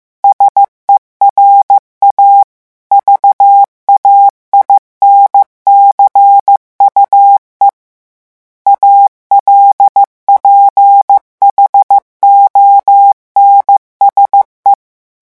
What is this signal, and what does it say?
Morse code, heard as a keyed tone, about 15 wpm, 800 Hz. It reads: SERA VAINCUE ALPHONSE
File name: morse mp3 / revision9 citations humoristiques vitesse 14 mots minute